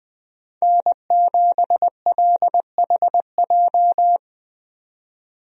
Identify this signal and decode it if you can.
Morse code, heard as a keyed tone, about 20 wpm, 700 Hz. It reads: N7LHJ
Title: USA Call Signs - 5 Characters - 20wpm